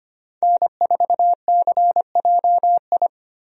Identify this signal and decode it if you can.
Morse code, heard as a keyed tone, about 25 wpm, 700 Hz. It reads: N4CJI
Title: USA Call Signs - 5 Characters - Repeated 2x - 25wpm